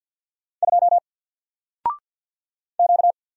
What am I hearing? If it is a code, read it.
Morse code, heard as a keyed tone, about 48 wpm, 700 Hz. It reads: 2 Q
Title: Single Letter-Number Alt 50wpm